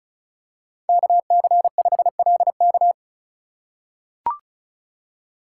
Morse code, heard as a keyed tone, about 36 wpm, 700 Hz. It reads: KC5LK
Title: USA Call Signs - 5 Characters - 8x Character Spacing - 35wpm